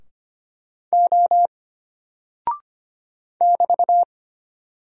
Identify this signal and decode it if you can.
Morse code, heard as a keyed tone, about 25 wpm, 700 Hz. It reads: O <BT>
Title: Single Character (Letter:Number:Punctuation:Prosign) Alt 25wpm